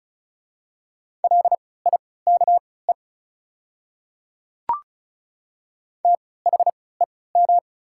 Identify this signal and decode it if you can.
Morse code, heard as a keyed tone, about 36 wpm, spaced as for 20 wpm, 700 Hz. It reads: LIKE THEM
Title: 3x Character Spacing - Top 100 Words - 35wpm